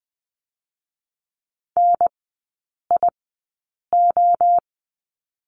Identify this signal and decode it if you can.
Morse code, heard as a keyed tone, about 19 wpm, 700 Hz. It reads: N I O
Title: 205 Lesson 33e - Letter j - Three Characters 20wpm